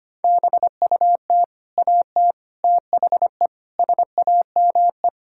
Morse code, heard as a keyed tone, about 25 wpm, 700 Hz. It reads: BUT AT THE SAME
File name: practice / 25wpm / Sentences with 4-Word Phrases - 25wpm